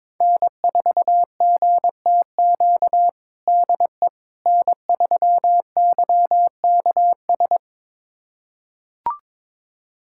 Morse code, heard as a keyed tone, about 22 wpm, 700 Hz. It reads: N4GTQ DE N3YKS
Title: USA Callsign DE Callsign - 5 Characters - 22wpm